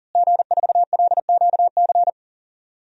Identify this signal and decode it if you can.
Morse code, heard as a keyed tone, about 40 wpm, 700 Hz. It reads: G4LQC
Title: DXCC-100 Call Signs - 5 Characters - Repeated 2x - 40wpm